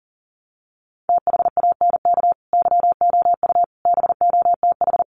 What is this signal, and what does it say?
Morse code, heard as a keyed tone, about 40 wpm, 700 Hz. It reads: THANK YOU BOTH
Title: Sets of 5 Words 40wpm